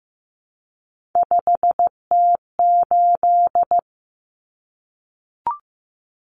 Morse code, heard as a keyed tone, about 15 wpm, 700 Hz. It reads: 5T8
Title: Common Cut Numbers - 3-Digit 15wpm